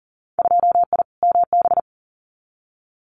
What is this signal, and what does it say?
Morse code, heard as a keyed tone, about 40 wpm, 700 Hz. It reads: 2I MB
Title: Winter Field Day Exchanges Encore 40wpm